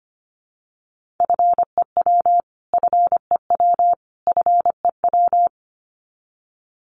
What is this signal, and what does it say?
Morse code, heard as a keyed tone, about 25 wpm, 700 Hz. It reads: FEW FEW FEW
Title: Top 500 Words Repeated 25wpm